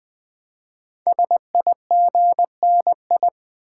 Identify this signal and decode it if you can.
Morse code, heard as a keyed tone, about 20 wpm, 700 Hz. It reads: SIGNI
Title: Build A Word - 5 Letters At A Time - 12-Letter Words - 20wpm